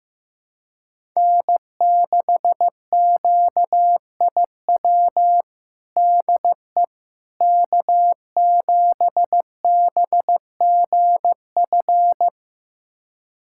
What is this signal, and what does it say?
Morse code, heard as a keyed tone, about 15 wpm, 700 Hz. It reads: N6QIW DE K7BGF
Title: USA Callsign DE Callsign - 5 Characters - 15wpm